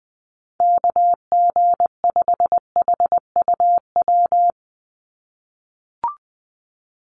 Morse code, heard as a keyed tone, about 20 wpm, 700 Hz. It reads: KG5HUW
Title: USA Call Signs 20wpm